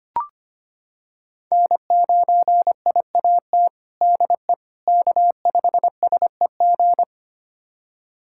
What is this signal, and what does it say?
Morse code, heard as a keyed tone, about 25 wpm, 700 Hz. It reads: N9IAT DE K5SEG
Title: USA Callsign DE Callsign - 5 Characters - 25wpm